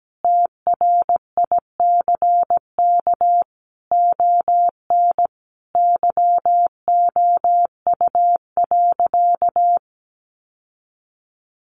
Morse code, heard as a keyed tone, about 17 wpm, 700 Hz. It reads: TRICK ON YOU.
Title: Sentences from Top 1500 Words 17wpm